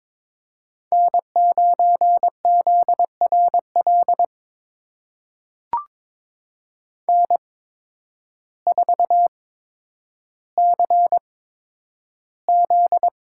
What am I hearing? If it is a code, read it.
Morse code, heard as a keyed tone, about 22 wpm, spaced as for 10 wpm, 700 Hz. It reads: N9ZRL N4CZ
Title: USA Call Signs - 8x Character Spacing - 22wpm